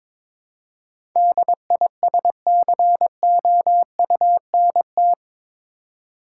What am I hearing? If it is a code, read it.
Morse code, heard as a keyed tone, about 22 wpm, 700 Hz. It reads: DISCOUNT
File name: practice / 22wpm / Eight Letter Words - 22wpm